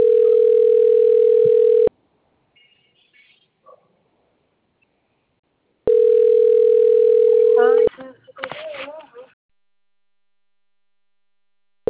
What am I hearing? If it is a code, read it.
Ringback tone (North America), 440 + 480 Hz, with speech.
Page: Bebedouro 3